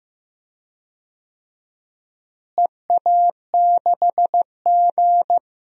Morse code, heard as a keyed tone, about 15 wpm, 700 Hz. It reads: EA6G
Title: DXCC-100 Call Signs - 4 Characters - Repeated 2x - 15wpm